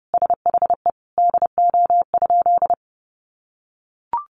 Morse code, heard as a keyed tone, about 30 wpm, 700 Hz. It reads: SHE DO?
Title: Sentences from Top 800 Words 30wpm